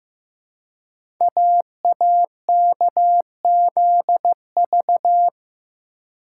Morse code, heard as a keyed tone, about 15 wpm, 700 Hz. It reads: AAKZV
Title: Code Groups - 5 Letter:Numbers - 15wpm